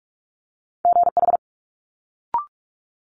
Morse code, heard as a keyed tone, about 46 wpm, 700 Hz. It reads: GH
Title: Multi-letter Phonemes Quick Fire 45wpm